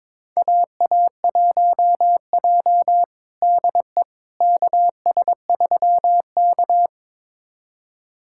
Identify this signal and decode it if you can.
Morse code, heard as a keyed tone, about 22 wpm, 700 Hz. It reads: AA1J DE KS3K
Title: USA Callsign DE Callsign - 4 Characters - 22wpm